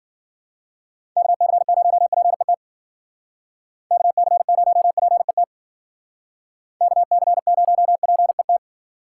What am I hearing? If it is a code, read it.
Morse code, heard as a keyed tone, about 48 wpm, 700 Hz. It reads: KC0PET KC0PET KC0PET
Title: Speed Racing - USA 2x3 Call Signs 40wpm